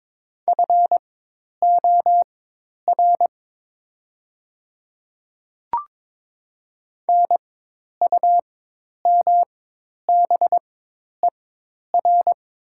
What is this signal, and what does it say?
Morse code, heard as a keyed tone, about 22 wpm, spaced as for 10 wpm, 700 Hz. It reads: FOR NUMBER
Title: 4x Character Spacing - Top 200 Words - 22wpm